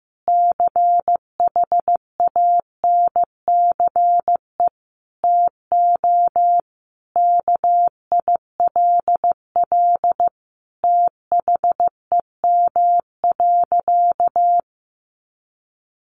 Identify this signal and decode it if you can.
Morse code, heard as a keyed tone, about 15 wpm, 700 Hz. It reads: CHANCE TO KILL THEM.
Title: Sentences from Top 1400 Words 15wpm